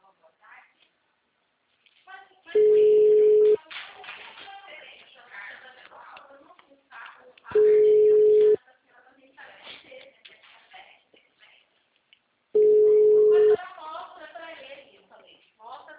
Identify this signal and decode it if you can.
Ringback tone (Europe, 425 Hz), with speech.
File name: out- -2001- - - .29439